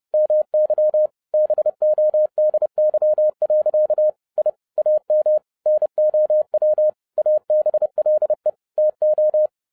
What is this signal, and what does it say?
Morse code, heard as a keyed tone, about 30 wpm, 610 Hz. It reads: MY BODY. I AM NOW ABLE TO
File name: 2011-07-mp3 / n60 BACK FROM A BIG ACCIDENT 30wpm